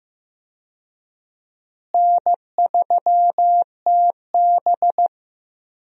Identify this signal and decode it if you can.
Morse code, heard as a keyed tone, about 15 wpm, 700 Hz. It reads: N3TB
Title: USA Call Signs - 4 Characters - Repeated 3x - 15wpm